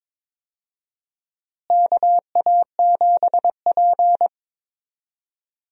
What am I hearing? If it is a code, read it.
Morse code, heard as a keyed tone, about 22 wpm, 700 Hz. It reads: KA7P
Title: USA Call Signs - 4 Characters - Repeated 3x - 22wpm